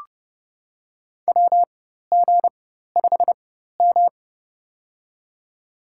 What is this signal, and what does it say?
Morse code, heard as a keyed tone, about 30 wpm, spaced as for 14 wpm, 700 Hz. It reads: WG5M
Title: USA Call Signs - 4x Character Spacing - 30wpm